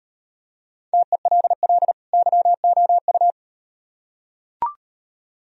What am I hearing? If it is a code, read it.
Morse code, heard as a keyed tone, about 38 wpm, 700 Hz. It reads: TELL YOU